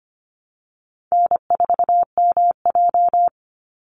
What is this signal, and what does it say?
Morse code, heard as a keyed tone, about 25 wpm, 700 Hz. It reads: N4MJ
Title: NA SOTA Chaser Callsigns 25wpm